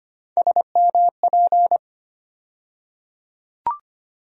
Morse code, heard as a keyed tone, about 25 wpm, 700 Hz. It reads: SMP
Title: USA Call Sign Postfixes - 3 Letters 25wpm